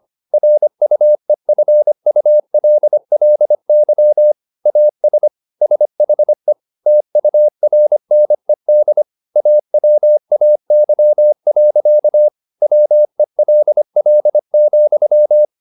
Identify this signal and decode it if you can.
Morse code, heard as a keyed tone, about 25 wpm, 600 Hz. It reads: RUEFULLY AS SHE TURNED AWAY. WELL,